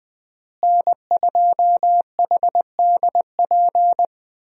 Morse code, heard as a keyed tone, about 20 wpm, 700 Hz. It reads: N2HDP
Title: USA Call Signs - 5 Characters - Repeated 2x - 20wpm